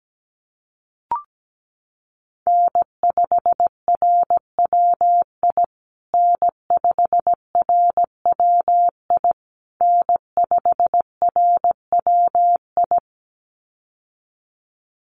Morse code, heard as a keyed tone, about 17 wpm, 700 Hz. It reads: N5RWI N5RWI N5RWI
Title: USA Call Signs Repeated 3x 17wpm